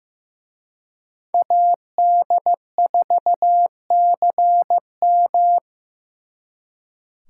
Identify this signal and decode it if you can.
Morse code, heard as a keyed tone, about 18 wpm, 700 Hz. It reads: AD4CM
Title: Speed Racing - USA 2x2 Call Signs 15wpm